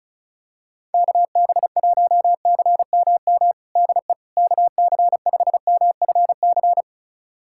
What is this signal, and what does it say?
Morse code, heard as a keyed tone, about 36 wpm, 700 Hz. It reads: KB1CMM DE KC5MFC
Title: USA Callsign DE Callsign - 6 Characters - 35wpm